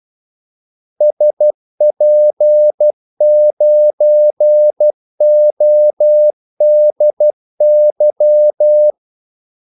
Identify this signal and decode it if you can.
Morse code, heard as a keyed tone, about 12 wpm, 600 Hz. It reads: SP9ODY
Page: 2 ekstra ordmellemrum mellem kaldesignalerne